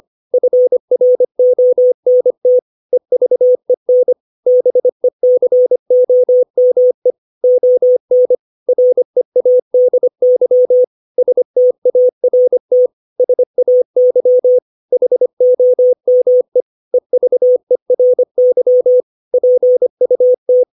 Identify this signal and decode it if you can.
Morse code, heard as a keyed tone, about 25 wpm, 500 Hz. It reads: FRONT EVEN BECOME ON READY START SAY HOME EVERY PUT
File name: 632 words / 632 Words 25WPM